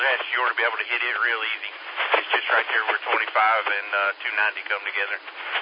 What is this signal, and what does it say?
Emergency Alert System (SAME) tones, heard as harsh buzzing bursts, with speech. It is heard from 0.8 to 4.7 s.
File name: SimplexLogic